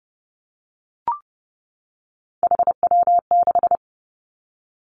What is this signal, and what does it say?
Morse code, heard as a keyed tone, about 32 wpm, 700 Hz. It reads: HW6
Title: Three Challenging Letter-Number Combinations 30wpm